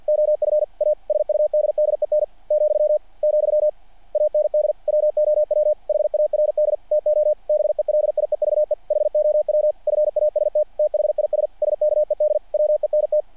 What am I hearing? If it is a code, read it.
Morse code, heard as a keyed tone, about 48 wpm, 600 Hz. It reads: OF A SUDDEN , , AND WOW HARD TO BELIEVE HOW FAST THIS SKED WENT
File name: archived / CRSnet